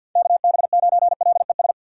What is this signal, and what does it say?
Morse code, heard as a keyed tone, about 49 wpm, 700 Hz. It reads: KB9LES
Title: USA Call Signs - 6 Characters - 4x Character Spacing - 50wpm